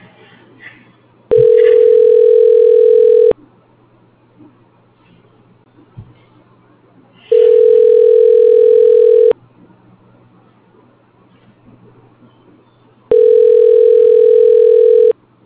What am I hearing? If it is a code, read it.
Ringback tone (North America), 440 + 480 Hz.